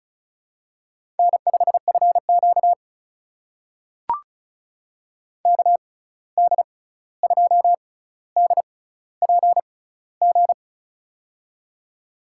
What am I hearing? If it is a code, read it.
Morse code, heard as a keyed tone, about 36 wpm, spaced as for 16 wpm, 700 Hz. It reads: N5FQ KD2DPG
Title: USA Call Signs - 6x Character Spacing - 35wpm